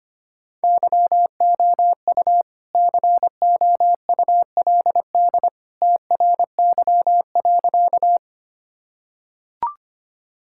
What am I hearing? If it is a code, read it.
Morse code, heard as a keyed tone, about 25 wpm, 700 Hz. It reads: YOU COULD TRY.
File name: practice / 25wpm / Sentences with 2-Word Phrases - 25wpm